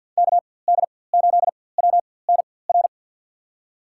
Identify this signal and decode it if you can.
Morse code, heard as a keyed tone, about 51 wpm, spaced as for 23 wpm, 700 Hz. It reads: KD8WNR
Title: USA Call Signs - 6 Characters - 4x Character Spacing - 50wpm